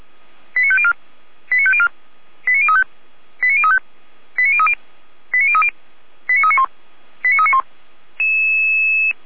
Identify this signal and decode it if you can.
Five-tone selective call (ZVEI), heard as quick runs of five tones, tones of about 70 ms. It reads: 89504 89504 89035 89035 89030 89030 89391 89391
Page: audio.f85360.feuerwehr-orlamuende.org